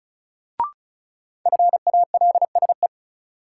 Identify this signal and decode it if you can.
Morse code, heard as a keyed tone, about 36 wpm, 700 Hz. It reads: FALSE
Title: Sentences from Top 1600 Words - New Word Drill - 35wpm